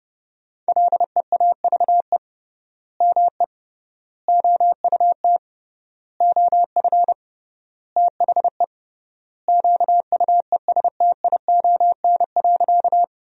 Morse code, heard as a keyed tone, about 32 wpm, 700 Hz. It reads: LEAVE ME OUT OF THE QUESTION.
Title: 3x Word-Spacing - Sentences from Top 400 Words - 30wpm